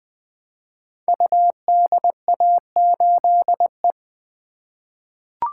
Morse code, heard as a keyed tone, about 20 wpm, 700 Hz. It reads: UDA8E
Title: Code Groups - 5 Letter:Numbers - 20wpm